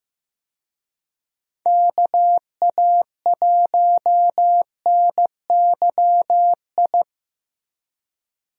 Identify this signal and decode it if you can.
Morse code, heard as a keyed tone, about 15 wpm, 700 Hz. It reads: KA1NYI
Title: USA Call Signs - 6 Characters - Repeated 2x - 15wpm